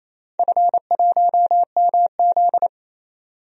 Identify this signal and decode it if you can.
Morse code, heard as a keyed tone, about 27 wpm, 700 Hz. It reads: F1MZ
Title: DXCC-100 Call Signs - 4 Characters - 28wpm